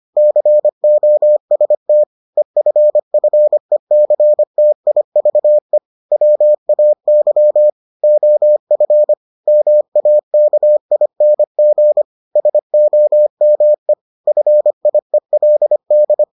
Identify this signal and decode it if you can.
Morse code, heard as a keyed tone, about 25 wpm, 600 Hz. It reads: COST EFFECTIVE WAY OF MAKING SOME FIELD